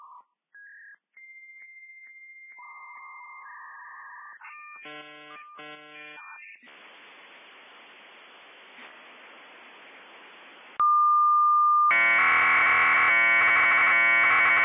Dial-up modem handshake.